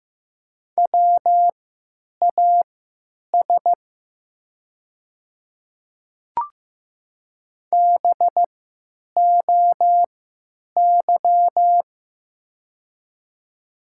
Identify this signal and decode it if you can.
Morse code, heard as a keyed tone, about 15 wpm, spaced as for 9 wpm, 700 Hz. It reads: WAS BOY
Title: 3x Character Spacing - Top 300 Words - 15wpm